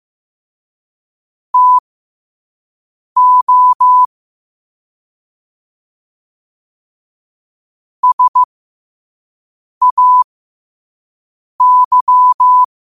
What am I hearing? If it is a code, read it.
Morse code, heard as a keyed tone, about 15 wpm, spaced as for 5 wpm, 1000 Hz. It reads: TO SAY
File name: 5wpm